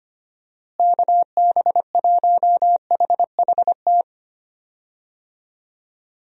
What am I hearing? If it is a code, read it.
Morse code, heard as a keyed tone, about 25 wpm, 700 Hz. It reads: KB1HHT
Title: USA Call Signs - 6 Characters - Repeated 3x - 25wpm